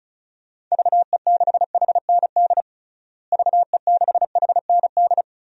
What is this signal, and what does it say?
Morse code, heard as a keyed tone, about 36 wpm, 700 Hz. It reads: VE6HND VE6HND
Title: DXCC-100 Call Signs - 6 Characters - Repeated 2x - 35wpm